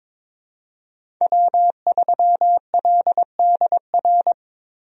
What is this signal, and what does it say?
Morse code, heard as a keyed tone, about 22 wpm, 700 Hz. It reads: W3LDR
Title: USA Call Signs - 5 Characters - Repeated 3x - 22wpm